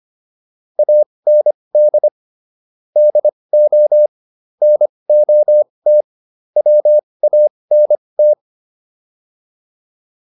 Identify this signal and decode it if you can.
Morse code, heard as a keyed tone, about 25 wpm, spaced as for 20 wpm, 600 Hz. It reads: AND DO NOT WANT